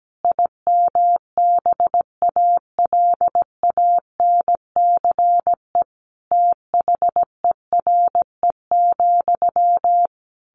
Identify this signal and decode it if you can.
Morse code, heard as a keyed tone, about 17 wpm, 700 Hz. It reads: IMBALANCE THERE,
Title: A Sound Of Thunder 17wpm